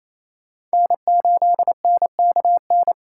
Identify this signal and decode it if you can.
Morse code, heard as a keyed tone, about 35 wpm, 700 Hz. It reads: N8NKN